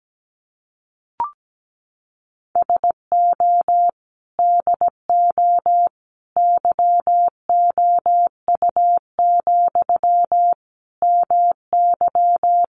Morse code, heard as a keyed tone, about 17 wpm, 700 Hz. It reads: SO DO YOU, MY
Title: Sentences from Top 400 Words 17wpm